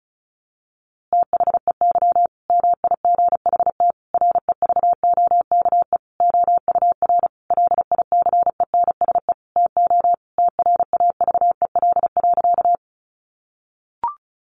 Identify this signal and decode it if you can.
Morse code, heard as a keyed tone, about 34 wpm, 700 Hz. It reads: THEY MIGHT REVOKE OUR LICENSE TO TRAVEL.
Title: A Sound Of Thunder 35wpm